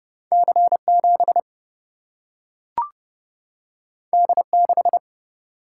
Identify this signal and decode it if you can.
Morse code, heard as a keyed tone, about 29 wpm, 700 Hz. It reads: C7 D6
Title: Single-Letter Single-Number 30wpm